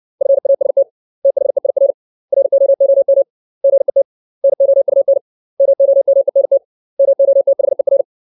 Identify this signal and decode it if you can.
Morse code, heard as a keyed tone, about 60 wpm, 550 Hz. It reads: FAST THEIR ROOM GET NOUN MOMENT MOTHER
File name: 632 words / 632 Words 60WPM